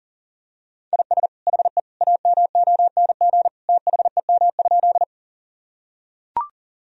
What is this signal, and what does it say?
Morse code, heard as a keyed tone, about 40 wpm, 700 Hz. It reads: IS HE AMONG THEM?
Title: Sentences from Top 1000 Words - Review Part 3 - 40wpm